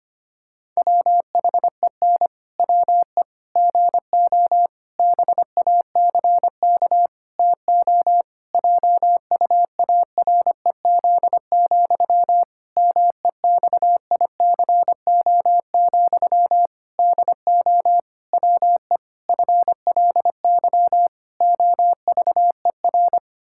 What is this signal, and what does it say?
Morse code, heard as a keyed tone, about 25 wpm, 700 Hz. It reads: WHEN WE GO BACK TO JUAREZ, MEXICO, DO WE FLY OVER
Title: Pangram Challenge v2 - 25wpm